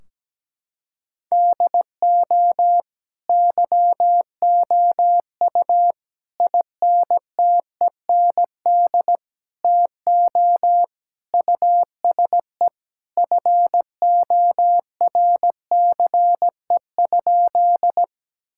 Morse code, heard as a keyed tone, about 17 wpm, 700 Hz. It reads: DO YOU INTEND TO USE FORCE?
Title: Sentences from Top 1600 Words 17wpm